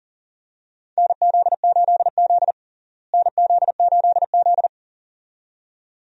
Morse code, heard as a keyed tone, about 40 wpm, 700 Hz. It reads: NZ8Z NZ8Z
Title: USA Call Signs - 4 Characters - Repeated 2x - 40wpm